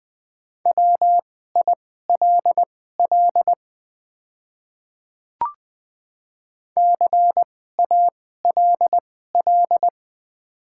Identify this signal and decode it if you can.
Morse code, heard as a keyed tone, about 20 wpm, spaced as for 14 wpm, 700 Hz. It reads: WILL CALL
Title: 2x Character Spacing - Top 100 Words - 20wpm